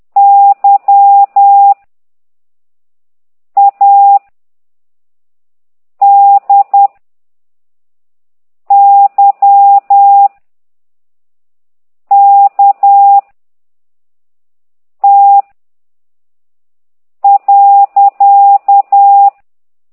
Morse code, heard as a keyed tone, about 10 wpm, 800 Hz. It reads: Y A D Y K T .
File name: ct2idw / exerc13